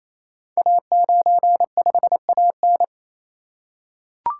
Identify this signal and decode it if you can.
Morse code, heard as a keyed tone, about 27 wpm, 700 Hz. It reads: A95AN
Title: Code Groups - 5 Letter:Numbers - 28wpm